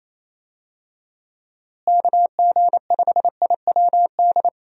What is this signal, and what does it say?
Morse code, heard as a keyed tone, about 27 wpm, 700 Hz. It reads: KG5IWD
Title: USA Call Signs - 6 Characters - Repeated 2x - 28wpm